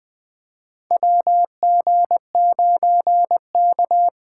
Morse code, heard as a keyed tone, about 20 wpm, 700 Hz. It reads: WG9K
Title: USA Call Signs - 4 Characters - 2x Character Spacing - 20wpm